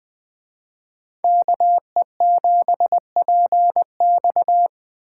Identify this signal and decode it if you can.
Morse code, heard as a keyed tone, about 20 wpm, 700 Hz. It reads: KE7PX
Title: Speed Racing - USA Call Signs 17wpm